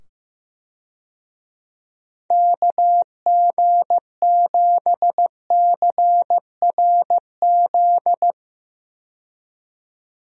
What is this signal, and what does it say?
Morse code, heard as a keyed tone, about 15 wpm, 700 Hz. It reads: KG7CRZ
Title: USA Call Signs - 6 Characters - 15wpm